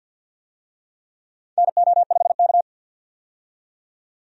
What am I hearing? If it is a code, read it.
Morse code, heard as a keyed tone, about 52 wpm, 700 Hz. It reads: NO5K
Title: USA Call Signs - 4 Characters - 50wpm